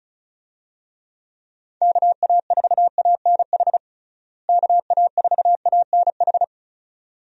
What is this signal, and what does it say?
Morse code, heard as a keyed tone, about 36 wpm, 700 Hz. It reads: KA4ANH KA4ANH
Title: USA Call Signs - 6 Characters - Repeated 2x - 35wpm